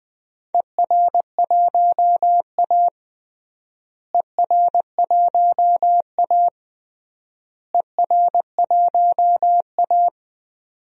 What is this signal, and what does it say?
Morse code, heard as a keyed tone, about 20 wpm, 700 Hz. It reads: ER1A ER1A ER1A
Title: DXCC-100 Call Signs - 4 Characters - Repeated 3x - 20wpm